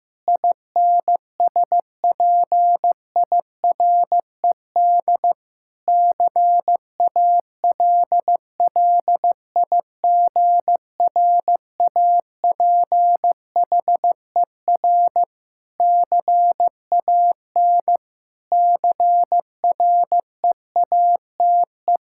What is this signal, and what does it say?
Morse code, heard as a keyed tone, about 15 wpm, 700 Hz. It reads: INSPIRED CALLIGRAPHER CAN CREATE
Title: Pangram Challenge v2 - 15wpm